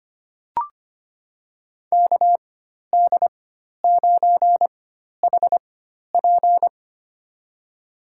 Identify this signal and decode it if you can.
Morse code, heard as a keyed tone, about 25 wpm, spaced as for 12 wpm, 700 Hz. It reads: KD9HP
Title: USA Call Signs - 5 Characters - 4x Character Spacing - 25wpm